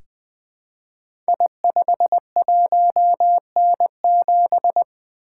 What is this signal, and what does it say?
Morse code, heard as a keyed tone, about 20 wpm, 700 Hz. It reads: I51N7
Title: Code Groups - 5 Letter:Numbers - 20wpm